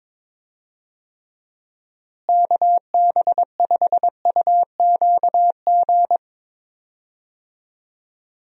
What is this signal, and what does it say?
Morse code, heard as a keyed tone, about 22 wpm, 700 Hz. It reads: KB5UQG
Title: USA Call Signs - 6 Characters - Repeated 3x - 22wpm